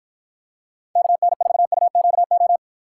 Morse code, heard as a keyed tone, about 50 wpm, 700 Hz. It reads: KN4UQO